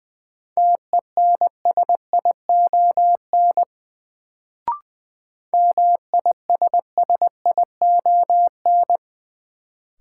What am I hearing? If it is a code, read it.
Morse code, heard as a keyed tone, about 20 wpm, 700 Hz. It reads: TENSION MISSION
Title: Sentences from Top 1600 Words - New Word Drill Alt - 20wpm